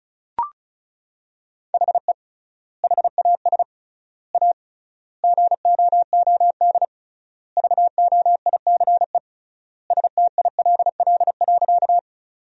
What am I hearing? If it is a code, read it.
Morse code, heard as a keyed tone, about 36 wpm, 700 Hz. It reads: HE HAS A GOOD VOICE STILL.
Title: 3x Word-Spacing - Sentences from Top 500 Words - 35wpm